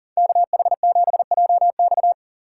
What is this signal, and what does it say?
Morse code, heard as a keyed tone, about 48 wpm, 700 Hz. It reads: KH7JX
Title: Speed Racing - USA 2x2 Call Signs 40wpm